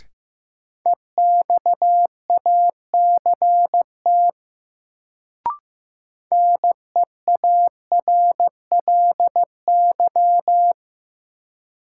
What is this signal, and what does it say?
Morse code, heard as a keyed tone, about 16 wpm, 700 Hz. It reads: EXACT NEARLY
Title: Sentences from Top 1300 Words - New Word Drill - 15wpm